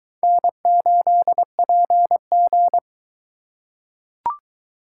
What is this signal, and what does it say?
Morse code, heard as a keyed tone, about 22 wpm, 700 Hz. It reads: N8PG